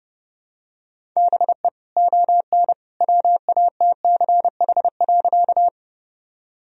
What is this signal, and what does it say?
Morse code, heard as a keyed tone, about 30 wpm, 700 Hz. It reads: BE ON WATCH.
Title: Sentences from Top 1000 Words - Review Part 1 - 30wpm